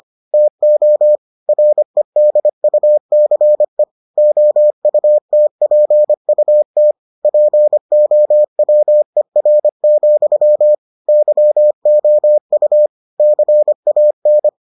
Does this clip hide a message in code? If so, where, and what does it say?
Yes: Morse code from 0.3 to 14.5 s, heard as a keyed tone, about 25 wpm, 600 Hz. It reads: TO REDUCE OUTPUT POWER, YOU CAN